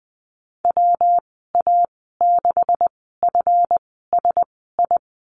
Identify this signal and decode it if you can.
Morse code, heard as a keyed tone, about 20 wpm, spaced as for 15 wpm, 700 Hz. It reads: WA6FSI
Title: 193 Lesson 31g - Letter v - Call Signs 20wpm